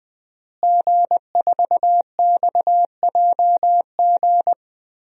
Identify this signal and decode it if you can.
Morse code, heard as a keyed tone, about 21 wpm, 700 Hz. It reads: G4XJG
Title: DXCC-100 Call Signs - 5 Characters - Repeated 3x - 20wpm